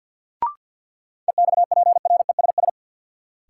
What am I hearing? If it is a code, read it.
Morse code, heard as a keyed tone, about 48 wpm, 700 Hz. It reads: EXPRESS
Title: Sentences from Top 1100 Words - New Word Drill Alt - 50wpm